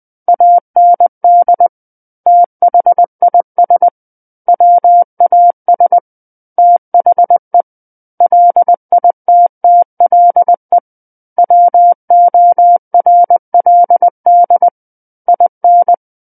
Morse code, heard as a keyed tone, about 20 wpm, 700 Hz. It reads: AND THIS WAS THE LITTLE WORLD IN
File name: 20 WPM